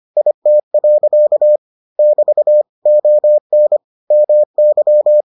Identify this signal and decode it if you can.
Morse code, heard as a keyed tone, about 25 wpm, 600 Hz. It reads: IT. <BT> ON MY